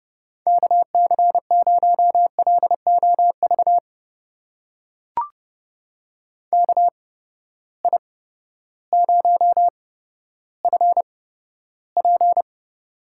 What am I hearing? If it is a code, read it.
Morse code, heard as a keyed tone, about 30 wpm, spaced as for 14 wpm, 700 Hz. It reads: KC0LOV KI0FP
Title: USA Call Signs - 8x Character Spacing - 30wpm